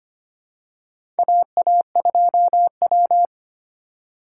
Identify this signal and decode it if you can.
Morse code, heard as a keyed tone, about 25 wpm, 700 Hz. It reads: AA2W
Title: USA Call Signs - 4x Character Spacing - 25wpm